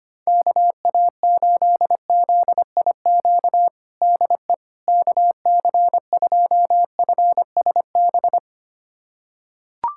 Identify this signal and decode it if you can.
Morse code, heard as a keyed tone, about 25 wpm, 700 Hz. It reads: KA8ZIQ DE KC2FSB
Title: USA Callsign DE Callsign 25wpm